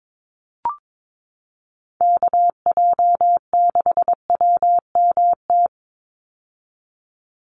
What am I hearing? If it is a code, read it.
Morse code, heard as a keyed tone, about 22 wpm, 700 Hz. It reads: KJ6WMT
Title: USA Call Signs 22wpm